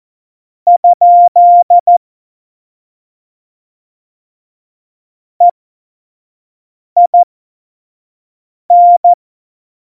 Morse code, heard as a keyed tone, about 14 wpm, spaced as for 5 wpm, 700 Hz. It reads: ? EIN
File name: Morsekurs / Witze / Witz 07 Juli 2020, 25 F70